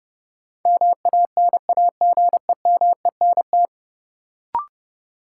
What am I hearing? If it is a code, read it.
Morse code, heard as a keyed tone, about 30 wpm, 700 Hz. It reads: MANAGEMENT
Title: Sentences from Top 1000 Words - Review Part 4 - New Word Drill - 30wpm